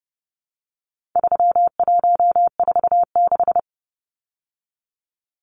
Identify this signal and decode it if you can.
Morse code, heard as a keyed tone, about 29 wpm, 700 Hz. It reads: 3146
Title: Numbers - 4-Digit 30wpm